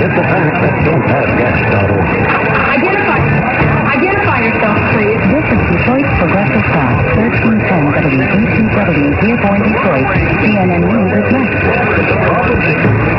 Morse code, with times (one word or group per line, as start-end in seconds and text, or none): none